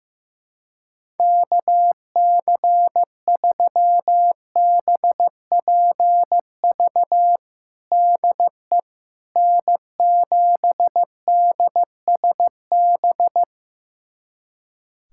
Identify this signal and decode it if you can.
Morse code, heard as a keyed tone, about 15 wpm, 700 Hz. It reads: KC3BPV DE N7DSB
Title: USA Callsign DE Callsign 15wpm